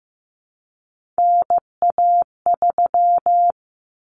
Morse code, heard as a keyed tone, about 15 wpm, 700 Hz. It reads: NA3
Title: Common Cut Numbers - 3-Digit 15wpm